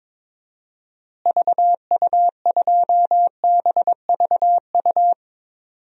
Morse code, heard as a keyed tone, about 21 wpm, 700 Hz. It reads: VU2BVU
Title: DXCC-100 Call Signs - 6 Characters - Repeated 3x - 22wpm